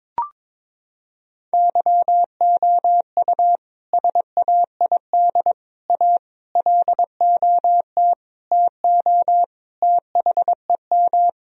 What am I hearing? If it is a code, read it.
Morse code, heard as a keyed tone, about 22 wpm, 700 Hz. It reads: YOU SAID A LOT TO THEM
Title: Sentences from Top 1000 Words - Review Part 3 - 22wpm